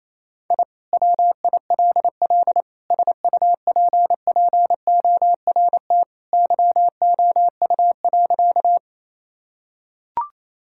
Morse code, heard as a keyed tone, about 27 wpm, 700 Hz. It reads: I WILL SUPPORT YOU.
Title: Sentences from Top 1000 Words - Review Part 5 - 28wpm